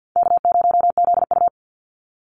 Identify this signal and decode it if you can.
Morse code, heard as a keyed tone, about 50 wpm, 700 Hz. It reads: K0ZU
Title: Easy Call Signs - 50wpm